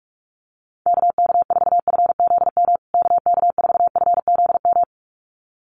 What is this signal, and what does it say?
Morse code, heard as a keyed tone, about 44 wpm, 700 Hz. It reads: KK4FZM KK4FZM
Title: USA Call Signs Repeated 2x 45wpm